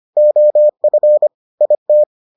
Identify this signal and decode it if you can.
Morse code, heard as a keyed tone, about 25 wpm, 600 Hz. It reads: OF IT